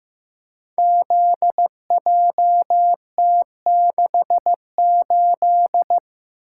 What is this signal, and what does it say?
Morse code, heard as a keyed tone, about 15 wpm, 700 Hz. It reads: ZJT68
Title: Code Groups - 5 Letter:Numbers - 15wpm